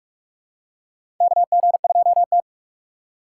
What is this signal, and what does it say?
Morse code, heard as a keyed tone, about 45 wpm, 700 Hz. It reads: KG2T
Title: USA Call Signs - 4 Characters - 4x Character Spacing - 45wpm